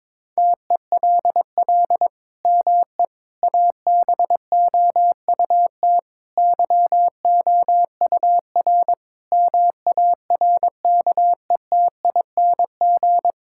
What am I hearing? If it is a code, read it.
Morse code, heard as a keyed tone, about 22 wpm, 700 Hz. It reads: TELL ME ABOUT YOUR MARKETING
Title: Sentences from Top 1000 Words - Review Part 5 - 22wpm